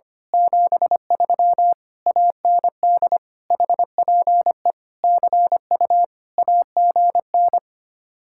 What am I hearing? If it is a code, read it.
Morse code, heard as a keyed tone, about 24 wpm, 700 Hz. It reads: 73 AND HPE CU AGN
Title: Speed-Racing Common QSO Phrases Encore - 25wpm